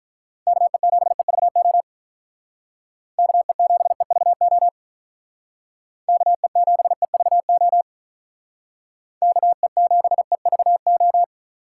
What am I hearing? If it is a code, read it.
Morse code, heard as a keyed tone, about 45 wpm, 700 Hz. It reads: KE7EVO KE7EVO KE7EVO KE7EVO